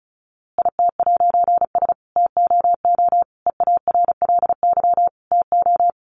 Morse code, heard as a keyed tone, about 34 wpm, 700 Hz. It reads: IT'S TOO EARLY TO
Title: A Sound Of Thunder 35wpm